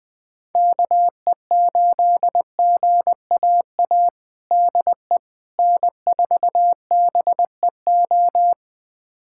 Morse code, heard as a keyed tone, about 20 wpm, 700 Hz. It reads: KE8GAA DE N4BEO
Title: USA Callsign DE Callsign 20wpm